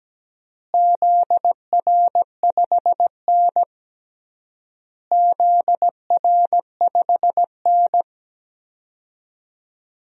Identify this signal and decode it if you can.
Morse code, heard as a keyed tone, about 17 wpm, 700 Hz. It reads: ZR5N ZR5N
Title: DXCC-100 Call Signs - 4 Characters - Repeated 3x - 17wpm